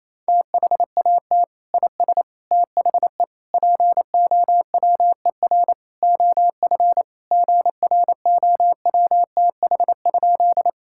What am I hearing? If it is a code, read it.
Morse code, heard as a keyed tone, about 28 wpm, 700 Hz. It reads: THAT IS THE POWER OF GROWTH?
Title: Sentences from Top 1000 Words - Review Part 5 - 28wpm